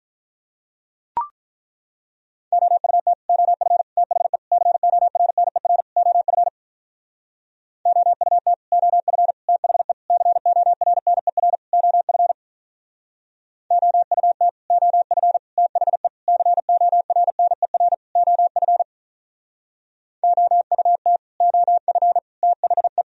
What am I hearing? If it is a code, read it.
Morse code, heard as a keyed tone, about 44 wpm, 700 Hz. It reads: OUT OF THE CORNER OF OUT OF THE CORNER OF OUT OF THE CORNER OF OUT OF THE
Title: Speed-Racing - Sets of 5 Words 35wpm